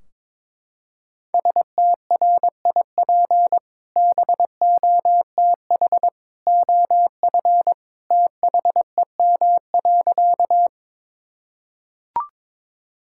Morse code, heard as a keyed tone, about 22 wpm, 700 Hz. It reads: STRIP BOTH OF THEM.
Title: Sentences from Top 1600 Words 22wpm